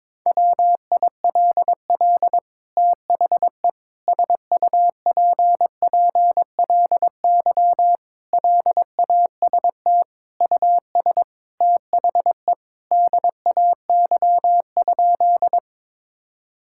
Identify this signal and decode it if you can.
Morse code, heard as a keyed tone, about 22 wpm, 700 Hz. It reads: WILL THE SUPPLY LAST US THE DAY?
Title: Sentences from Top 1000 Words - Review Part 5 - 22wpm